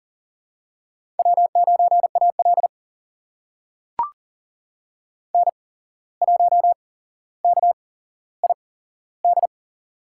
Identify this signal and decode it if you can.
Morse code, heard as a keyed tone, about 40 wpm, spaced as for 16 wpm, 700 Hz. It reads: W9AL N1KID
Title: USA Call Signs - 8x Character Spacing - 40wpm